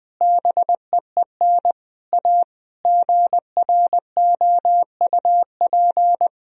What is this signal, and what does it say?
Morse code, heard as a keyed tone, about 20 wpm, 700 Hz. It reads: BEEN A GROUP
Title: Sentences from Top 1000 Words - Review Part 1 - 20wpm